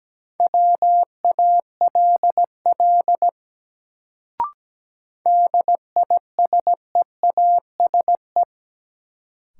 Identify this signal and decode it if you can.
Morse code, heard as a keyed tone, about 17 wpm, 700 Hz. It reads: WALL DISEASE
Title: Sentences from Top 1100 Words - New Word Drill Alt - 17wpm